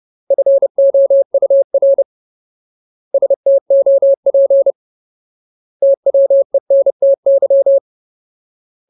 Morse code, heard as a keyed tone, about 30 wpm, 550 Hz. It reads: FOUR STOP TWENTY
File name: Ultimate Word List / 30WPM Ultimate